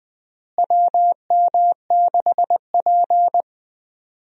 Morse code, heard as a keyed tone, about 20 wpm, 700 Hz. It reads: WM6P
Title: USA Call Signs - 4 Characters - Repeated 3x - 20wpm